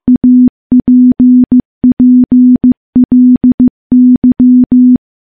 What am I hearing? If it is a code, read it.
Morse code, heard as a keyed tone, about 15 wpm, 250 Hz. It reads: APPLY
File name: morse / morse-training-15WPM